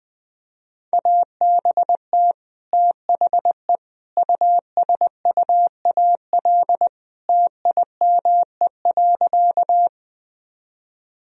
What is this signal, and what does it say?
Morse code, heard as a keyed tone, about 20 wpm, 700 Hz. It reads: ABT THE USUAL TIME.
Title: Sentences from Top 400 Words and Abbreviations 20wpm